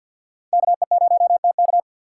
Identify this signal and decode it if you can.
Morse code, heard as a keyed tone, about 47 wpm, 700 Hz. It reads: KE0TK
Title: USA Call Signs - 5 Characters - 6x Character Spacing - 50wpm